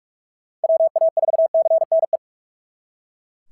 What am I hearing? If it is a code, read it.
Morse code, heard as a keyed tone, about 42 wpm, 690 Hz. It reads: WA4CNE
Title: SOTA QSO Practice 45wpm